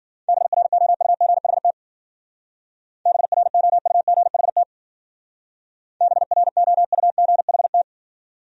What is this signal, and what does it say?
Morse code, heard as a keyed tone, about 48 wpm, 700 Hz. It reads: BROUGHT BROUGHT BROUGHT
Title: Speed-Racing Top 500 Words 40wpm